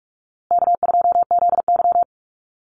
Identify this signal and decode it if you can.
Morse code, heard as a keyed tone, about 44 wpm, 700 Hz. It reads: K2ZY
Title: USA Call Signs Repeated 2x 45wpm